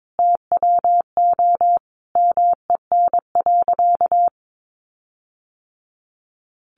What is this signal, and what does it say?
Morse code, heard as a keyed tone, about 22 wpm, 700 Hz. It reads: TWO MEN.
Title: Sentences from Top 1300 Words 22wpm